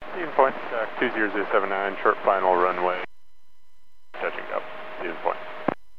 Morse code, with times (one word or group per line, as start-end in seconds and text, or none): none